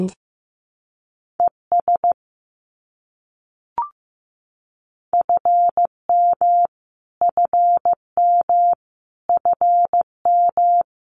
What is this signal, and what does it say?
Morse code, heard as a keyed tone, about 15 wpm, 700 Hz. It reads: ES FM FM FM
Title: Abbreviations Repeated 3x - 15wpm